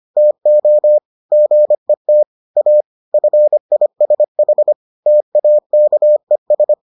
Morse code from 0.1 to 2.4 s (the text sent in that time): TO GET